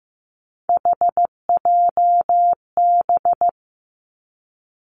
Morse code, heard as a keyed tone, about 15 wpm, 700 Hz. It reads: HJB
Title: Three Challenging Letter-Number Combinations 15wpm